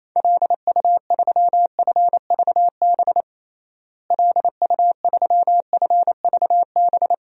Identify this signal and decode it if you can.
Morse code, heard as a keyed tone, about 28 wpm, 700 Hz. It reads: LU3FVB LU3FVB
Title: DXCC-100 Call Signs - 6 Characters - Repeated 3x - 28wpm